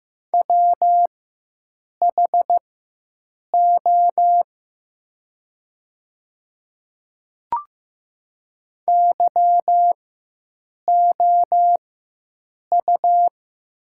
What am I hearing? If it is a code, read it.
Morse code, heard as a keyed tone, about 15 wpm, spaced as for 7 wpm, 700 Hz. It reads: WHO YOU
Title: 4x Character Spacing - Top 100 Words - 15wpm